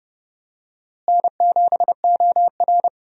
Morse code, heard as a keyed tone, about 32 wpm, 700 Hz. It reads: N7OR
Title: USA Call Signs - 4 Characters - Repeated 3x - 30wpm